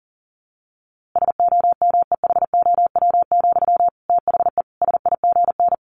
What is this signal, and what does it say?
Morse code, heard as a keyed tone, about 40 wpm, 700 Hz. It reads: SOMEHOW, THE SIGN
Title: A Sound Of Thunder 40wpm